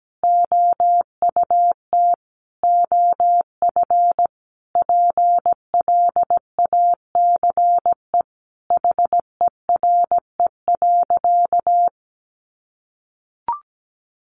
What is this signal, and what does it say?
Morse code, heard as a keyed tone, about 17 wpm, 700 Hz. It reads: OUT OF PLACE HERE.
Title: Sentences from Top 1500 Words 17wpm